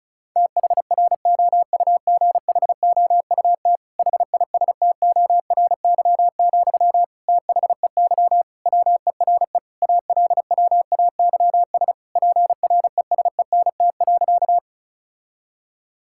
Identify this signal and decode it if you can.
Morse code, heard as a keyed tone, about 36 wpm, 700 Hz. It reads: THROUGHOUT HISTORY, THEY WERE ALWAYS PRESENT.
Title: Sentences from Top 1000 Words - Review Part 5 - 35wpm